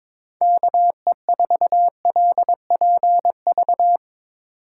Morse code, heard as a keyed tone, about 21 wpm, 700 Hz. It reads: KE4LPV
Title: USA Call Signs - 6 Characters - 3x Character Spacing - 22wpm